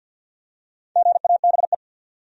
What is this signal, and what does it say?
Morse code, heard as a keyed tone, about 52 wpm, 700 Hz. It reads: GABE
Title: USA Common Names - Max 5 Letters - 50wpm